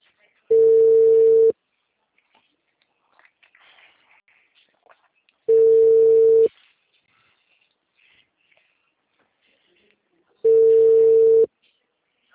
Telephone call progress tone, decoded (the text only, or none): ringback tone